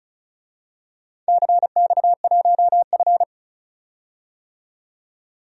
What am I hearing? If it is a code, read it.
Morse code, heard as a keyed tone, about 36 wpm, 690 Hz. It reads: CX1F
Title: DXCC-100 Call Signs - 4 Characters - 35wpm